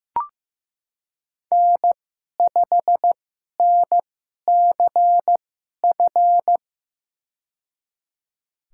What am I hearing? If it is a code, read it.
Morse code, heard as a keyed tone, about 15 wpm, spaced as for 11 wpm, 700 Hz. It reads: N5NCF
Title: USA Call Signs - 5 Characters - 2x Character Spacing - 15wpm